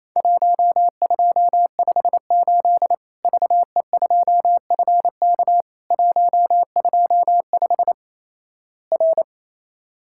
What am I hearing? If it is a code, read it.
Morse code, heard as a keyed tone, about 28 wpm, 700 Hz. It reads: 1258 VE2FK 125 R
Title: Worked All Europe DX Contest - QTC Exchange - 28wpm